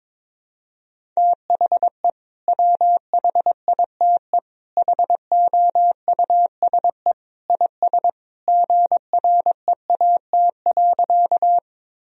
Morse code, heard as a keyed tone, about 22 wpm, 700 Hz. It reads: THE WHITE HOUSE IS GREAT.
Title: Sentences from Top 1000 Words - Review Part 1 - 22wpm